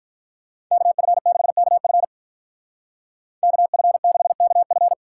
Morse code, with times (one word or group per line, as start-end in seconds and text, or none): 0.7-2.0 KF6KF
3.4-4.9 KF6KF